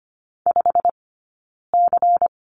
Morse code, heard as a keyed tone, about 25 wpm, spaced as for 9 wpm, 700 Hz. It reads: 5C
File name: course / 25:10wpm / 099 Lesson 17d - Letter u - Two Characters 25wpm